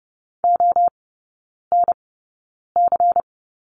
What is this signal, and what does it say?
Morse code, heard as a keyed tone, about 30 wpm, spaced as for 9 wpm, 700 Hz. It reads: ONC
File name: Lesson 16e - Letter c - Three Characters 30wpm